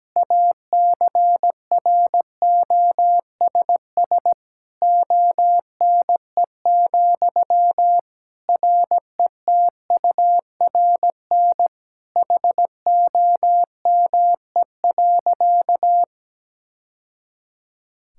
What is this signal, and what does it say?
Morse code, heard as a keyed tone, about 17 wpm, 700 Hz. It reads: ACROSS ONE, RETURN HOME.
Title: Sentences from Top 1000 Words - Review Part 5 - 17wpm